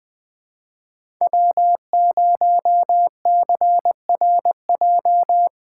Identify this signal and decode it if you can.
Morse code, heard as a keyed tone, about 20 wpm, 700 Hz. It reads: W0CRJ
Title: USA Call Signs - 2x Character Spacing - 20wpm